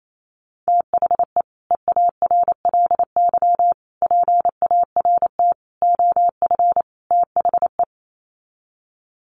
Morse code, heard as a keyed tone, about 28 wpm, 700 Hz. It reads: THE EARLY PART OF THE
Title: Sets of 5 Words Encore2 28wpm